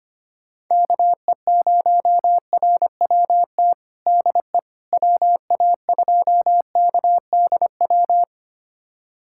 Morse code, heard as a keyed tone, about 25 wpm, 700 Hz. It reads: KE0RWT DE WA2KDW
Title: USA Callsign DE Callsign - 6 Characters - 25wpm